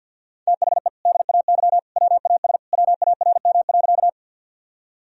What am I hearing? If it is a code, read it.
Morse code, heard as a keyed tone, about 50 wpm, 700 Hz. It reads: THE DAY WAS WARM.
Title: Sentences from Top 1000 Words - Review Part 2 - 50wpm